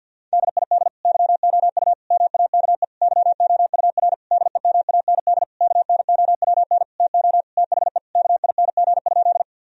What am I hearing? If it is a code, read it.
Morse code, heard as a keyed tone, about 50 wpm, 700 Hz. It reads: DID YOU MAKE YOUR DEMAND KNOWN TO THE KING?
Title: Sentences from Top 1000 Words - Review Part 4 - 50wpm